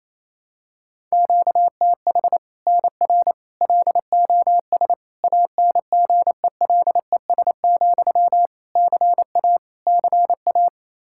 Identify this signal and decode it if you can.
Morse code, heard as a keyed tone, about 28 wpm, 700 Hz. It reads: QTH NR LOS ANGELES, CA CA
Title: Practice QSO - Exchange 1 - Beginning - 28wpm